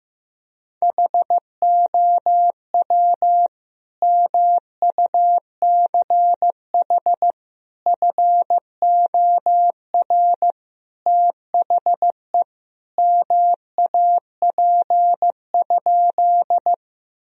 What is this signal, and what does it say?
Morse code, heard as a keyed tone, about 15 wpm, 700 Hz. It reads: HOW MUCH FOR THE MAP?
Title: Sentences from Top 1000 Words - Review Part 2 - 15wpm